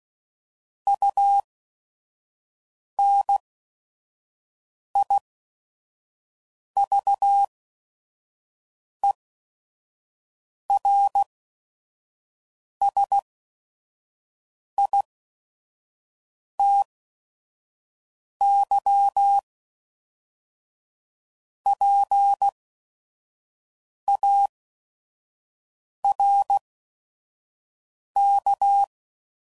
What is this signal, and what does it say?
Morse code, heard as a keyed tone, about 16 wpm, spaced as for 5 wpm, 800 Hz. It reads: UNIVERSITYPARK
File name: CW Code Groups @5WPM / fromSM's / Cities